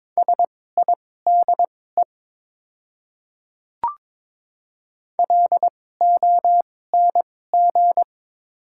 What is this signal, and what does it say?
Morse code, heard as a keyed tone, about 22 wpm, spaced as for 16 wpm, 700 Hz. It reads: SIDE LONG
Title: 2x Character Spacing - Top 100 Words - 22wpm